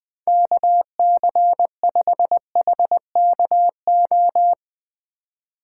Morse code, heard as a keyed tone, about 19 wpm, 700 Hz. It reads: KC5HKO
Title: USA Call Signs - 2x Character Spacing - 20wpm